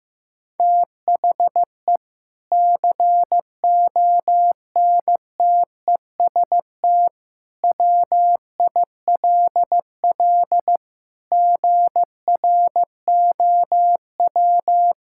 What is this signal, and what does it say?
Morse code, heard as a keyed tone, about 15 wpm, 700 Hz. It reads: THE CONTEST WILL GROW
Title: Sentences from Top 1600 Words 15wpm